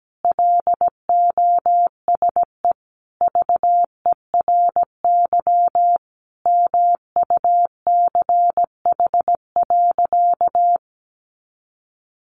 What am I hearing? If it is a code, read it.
Morse code, heard as a keyed tone, about 17 wpm, 700 Hz. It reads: LOSE VERY MUCH.
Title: Sentences from Top 1100 Words 17wpm